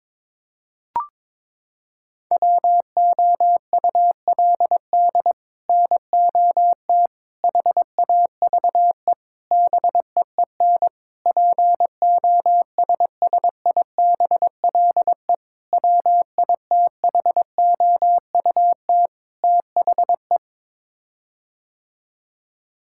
Morse code, heard as a keyed tone, about 22 wpm, 700 Hz. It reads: WOULD NOT HAVE BEEN POSSIBLE WITHOUT THE
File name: practice / 22wpm / Sets of 7 Words - 22wpm